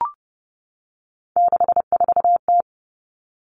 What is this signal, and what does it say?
Morse code, heard as a keyed tone, about 32 wpm, 700 Hz. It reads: 64T
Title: Three Challenging Letter-Number Combinations 30wpm